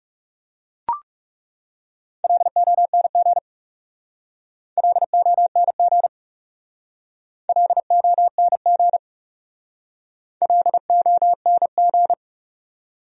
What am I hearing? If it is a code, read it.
Morse code, heard as a keyed tone, about 40 wpm, 700 Hz. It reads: LONG LONG LONG LONG